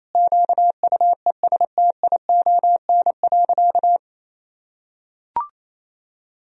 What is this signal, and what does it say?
Morse code, heard as a keyed tone, about 28 wpm, 700 Hz. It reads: QUESTION.
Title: Sentences from Top 1000 Words - Review Part 3 - 28wpm